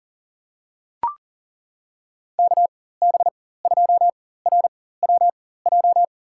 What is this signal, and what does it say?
Morse code, heard as a keyed tone, about 40 wpm, spaced as for 19 wpm, 700 Hz. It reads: KB2RWJ
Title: USA Call Signs - 6 Characters - 4x Character Spacing - 40wpm